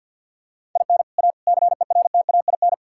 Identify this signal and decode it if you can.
Morse code, heard as a keyed tone, about 50 wpm, 700 Hz. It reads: IN A CERTAIN
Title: Sentences from Top 1000 Words - Review Part 6 - 50wpm